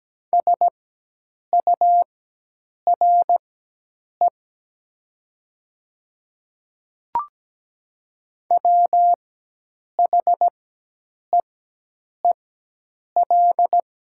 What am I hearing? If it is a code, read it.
Morse code, heard as a keyed tone, about 17 wpm, spaced as for 8 wpm, 700 Hz. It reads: SURE WHEEL
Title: 4x Character Spacing - Top 500 Words - 17wpm